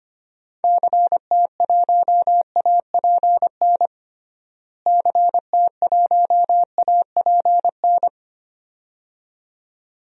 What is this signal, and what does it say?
Morse code, heard as a keyed tone, about 25 wpm, 700 Hz. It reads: CT1APN CT1APN
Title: DXCC-100 Call Signs - 6 Characters - Repeated 2x - 25wpm